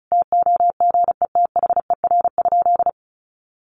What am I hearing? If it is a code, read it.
Morse code, heard as a keyed tone, about 34 wpm, 700 Hz. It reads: TOGETHER?
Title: Sentences from Top 1400 Words 35wpm